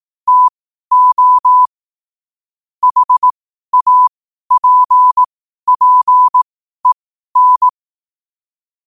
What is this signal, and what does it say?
Morse code, heard as a keyed tone, about 18 wpm, spaced as for 12 wpm, 1000 Hz. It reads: TO HAPPEN
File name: qotdfiles / 12wpm